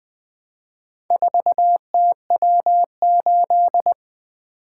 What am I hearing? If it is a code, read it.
Morse code, heard as a keyed tone, about 24 wpm, 700 Hz. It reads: 4TW8
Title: Speed-Racing - Four Letter-Number Combinations - 20wpm